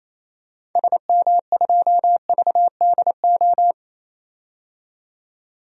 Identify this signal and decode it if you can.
Morse code, heard as a keyed tone, about 28 wpm, 700 Hz. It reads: SM2VDO
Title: DXCC-100 Call Signs - 6 Characters - 28wpm